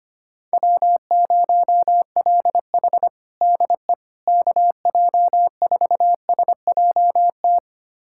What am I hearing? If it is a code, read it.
Morse code, heard as a keyed tone, about 25 wpm, 700 Hz. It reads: W0LH DE KJ4SJT
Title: USA Callsign DE Callsign 25wpm